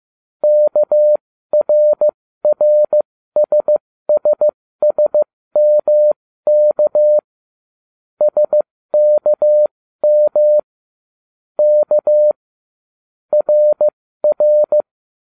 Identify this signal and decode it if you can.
Morse code, heard as a keyed tone, about 15 wpm, spaced as for 13 wpm, 600 Hz. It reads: KRRSSSMK SKM K RR